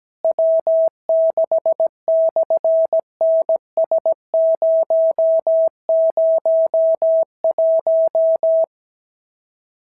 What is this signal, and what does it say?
Morse code, heard as a keyed tone, about 17 wpm, 650 Hz. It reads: W6/NS001
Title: SOTA QSO Practice 17wpm